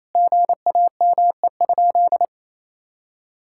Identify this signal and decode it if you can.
Morse code, heard as a keyed tone, about 28 wpm, 700 Hz. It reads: GAME?
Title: Sentences from Top 1600 Words 28wpm